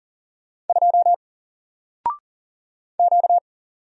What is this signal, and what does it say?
Morse code, heard as a keyed tone, about 40 wpm, 700 Hz. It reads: 2 Q
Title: Single Letter-Number 40wpm